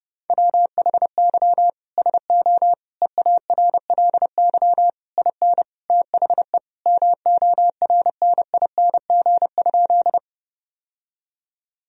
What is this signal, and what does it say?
Morse code, heard as a keyed tone, about 32 wpm, 700 Hz. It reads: WHY SO EARLY IN THE MORNING?
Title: Sentences from Top 1000 Words - Review Part 2 - 30wpm